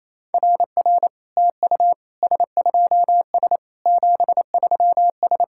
Morse code, heard as a keyed tone, about 28 wpm, 700 Hz. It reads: RR TU S2S 73S
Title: SOTA QSO Practice 28wpm